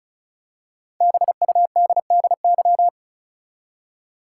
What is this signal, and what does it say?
Morse code, heard as a keyed tone, about 34 wpm, 700 Hz. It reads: BUDDY
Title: Five Letter Words - 35wpm